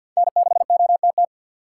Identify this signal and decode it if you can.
Morse code, heard as a keyed tone, about 50 wpm, 700 Hz. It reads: N6OTT
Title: USA Call Signs - 5 Characters - 3x Character Spacing - 50wpm